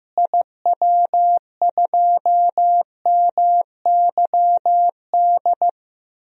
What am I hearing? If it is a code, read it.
Morse code, heard as a keyed tone, about 15 wpm, 700 Hz. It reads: IW2MYD
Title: DXCC-100 Call Signs - 6 Characters - Repeated 2x - 15wpm